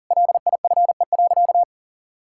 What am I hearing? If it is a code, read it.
Morse code, heard as a keyed tone, about 40 wpm, 700 Hz. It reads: LIFE.
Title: Sentences from Top 1000 Words - Review Part 6 - 40wpm